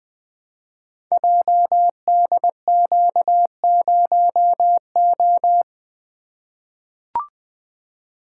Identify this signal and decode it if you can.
Morse code, heard as a keyed tone, about 20 wpm, 700 Hz. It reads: JDQ0O
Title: Code Groups - 5 Letter:Numbers - 20wpm